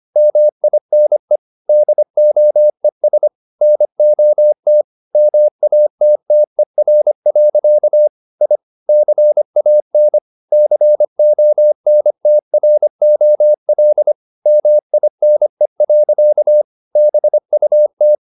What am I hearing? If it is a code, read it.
Morse code, heard as a keyed tone, about 25 wpm, 600 Hz. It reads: MINE DOES NOT MATTER. I CAN CONTROL MINE. BUT